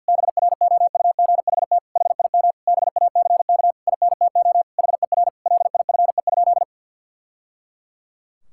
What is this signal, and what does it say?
Morse code, heard as a keyed tone, about 50 wpm, 700 Hz. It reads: BROUGHT HIM BACK INTO HER LIFE?
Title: Sentences from Top 1000 Words - Review Part 6 - 50wpm